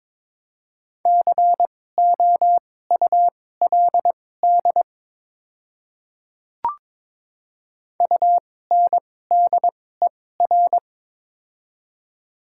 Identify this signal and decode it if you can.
Morse code, heard as a keyed tone, about 22 wpm, spaced as for 16 wpm, 700 Hz. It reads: COULD UNDER
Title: 2x Character Spacing - Top 500 Words - 22wpm